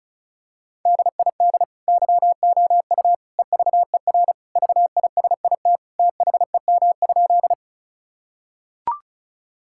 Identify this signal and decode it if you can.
Morse code, heard as a keyed tone, about 36 wpm, 700 Hz. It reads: DID YOU EVER VISIT THEM?
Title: Sentences from Top 1000 Words - Review Part 6 - 35wpm